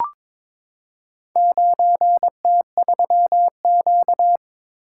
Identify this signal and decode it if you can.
Morse code, heard as a keyed tone, about 23 wpm, 700 Hz. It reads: 9T3Q
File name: practice / 22wpm / Four Letter-Number Combinations - 22wpm